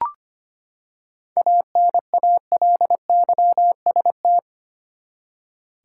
Morse code, heard as a keyed tone, about 25 wpm, 700 Hz. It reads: ANALYST
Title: Seven Letter Words - 25wpm